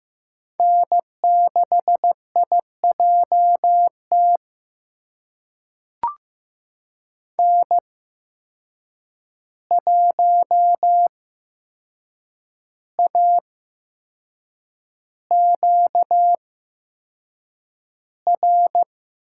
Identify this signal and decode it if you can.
Morse code, heard as a keyed tone, about 15 wpm, spaced as for 6 wpm, 700 Hz. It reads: N6IJT N1AQR
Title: USA Call Signs - 5 Characters - 8x Character Spacing - 15wpm